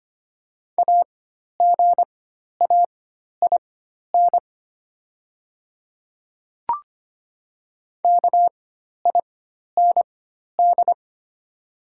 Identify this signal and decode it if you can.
Morse code, heard as a keyed tone, about 25 wpm, spaced as for 12 wpm, 690 Hz. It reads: AGAIN KIND
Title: 4x Character Spacing - Top 200 Words - 25wpm